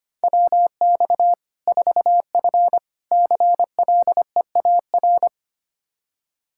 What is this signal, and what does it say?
Morse code, heard as a keyed tone, about 25 wpm, 700 Hz. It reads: WX 4F CLEAR
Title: Practice QSO - Exchange 2 - Info - 25wpm